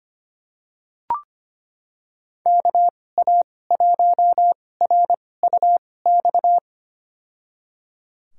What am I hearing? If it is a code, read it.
Morse code, heard as a keyed tone, about 25 wpm, spaced as for 18 wpm, 700 Hz. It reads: KA1RUX
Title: USA Call Signs - 2x Character Spacing - 25wpm